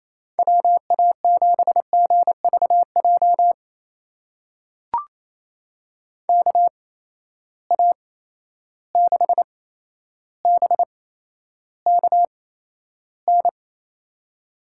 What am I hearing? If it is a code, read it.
Morse code, heard as a keyed tone, about 27 wpm, spaced as for 12 wpm, 700 Hz. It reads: WA7GVJ KA6BKN
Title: USA Call Signs - 6 Characters - 8x Character Spacing - 28wpm